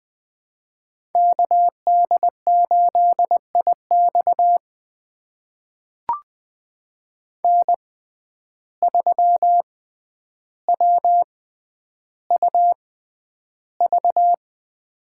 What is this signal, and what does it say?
Morse code, heard as a keyed tone, about 20 wpm, spaced as for 10 wpm, 700 Hz. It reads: KD8IX N3WUV
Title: USA Call Signs - 5 Characters - 6x Character Spacing - 20wpm